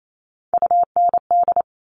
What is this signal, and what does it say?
Morse code, heard as a keyed tone, about 28 wpm, 700 Hz. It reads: UND
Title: Top Three Letter Combinations in Words 28wpm